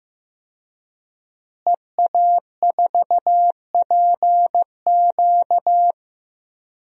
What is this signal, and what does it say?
Morse code, heard as a keyed tone, about 15 wpm, 700 Hz. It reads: EA4PQ
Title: DXCC-100 Call Signs - 5 Characters - Repeated 2x - 15wpm